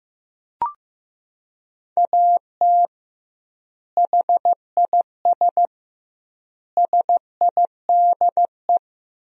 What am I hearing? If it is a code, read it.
Morse code, heard as a keyed tone, about 15 wpm, 700 Hz. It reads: AT HIS SIDE
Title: 2x Word-Spacing - Sentences from Top 500 Words - 15wpm